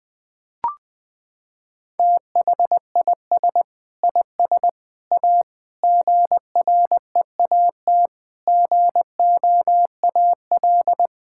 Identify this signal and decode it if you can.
Morse code, heard as a keyed tone, about 20 wpm, 700 Hz. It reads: THIS IS A GREAT GOAL
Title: Sentences from Top 1000 Words - Review Part 6 - 20wpm